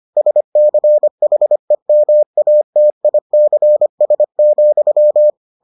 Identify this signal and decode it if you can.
Morse code, heard as a keyed tone, about 25 wpm, 600 Hz. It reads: SCHEMATICS,